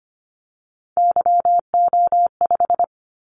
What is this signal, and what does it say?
Morse code, heard as a keyed tone, about 25 wpm, 700 Hz. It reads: YO5
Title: Callsign Prefixes for Top 25 DXCC - 25wpm